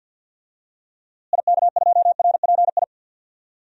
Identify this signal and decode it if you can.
Morse code, heard as a keyed tone, about 49 wpm, 700 Hz. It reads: IK2RPI
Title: DXCC-100 Call Signs - 6 Characters - Repeated 2x - 50wpm